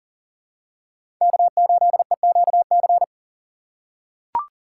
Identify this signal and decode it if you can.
Morse code, heard as a keyed tone, about 40 wpm, 700 Hz. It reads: K8EQC
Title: USA Call Signs - 5 Characters - 2x Character Spacing - 40wpm